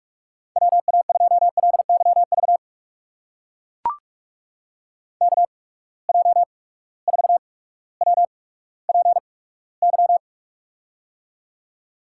Morse code, heard as a keyed tone, about 44 wpm, spaced as for 19 wpm, 700 Hz. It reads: WA2LYV KJ4WPY
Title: USA Call Signs - 8x Character Spacing - 45wpm